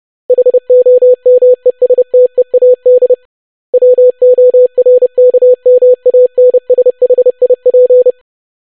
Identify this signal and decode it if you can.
Morse code, heard as a keyed tone, about 30 wpm, 500 Hz. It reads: HOMESTEAD WORKMANSHIP
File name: longwords-25wpm